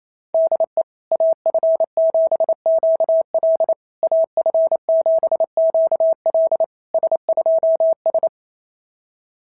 Morse code, heard as a keyed tone, about 27 wpm, 690 Hz. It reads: DE AF7QL AF7QL S2S
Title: SOTA QSO Practice 28wpm